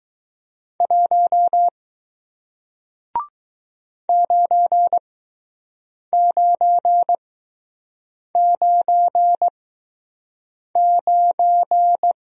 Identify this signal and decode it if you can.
Morse code, heard as a keyed tone, about 20 wpm, spaced as for 6 wpm, 700 Hz. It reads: 1 9999